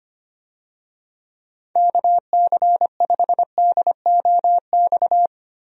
Morse code, heard as a keyed tone, about 25 wpm, 700 Hz. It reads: KC5DOX
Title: USA Call Signs - 6 Characters - 25wpm